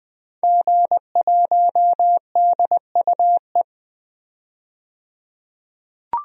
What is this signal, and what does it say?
Morse code, heard as a keyed tone, about 20 wpm, 700 Hz. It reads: G1DUE
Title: DXCC-100 Call Signs - 5 Characters - 20wpm